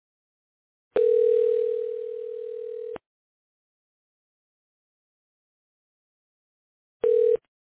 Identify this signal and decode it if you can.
Ringback tone (North America), 440 + 480 Hz.